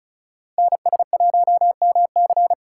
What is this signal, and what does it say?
Morse code, heard as a keyed tone, about 36 wpm, 700 Hz. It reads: NS1MC
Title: USA Call Signs - 5 Characters - 3x Character Spacing - 35wpm